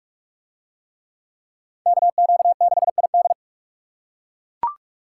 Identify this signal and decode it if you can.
Morse code, heard as a keyed tone, about 44 wpm, 700 Hz. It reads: KQ6ID